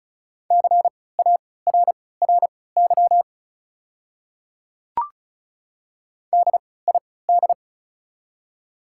Morse code, heard as a keyed tone, about 36 wpm, spaced as for 20 wpm, 690 Hz. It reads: CARRY DID
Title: 3x Character Spacing - Top 500 Words - 35wpm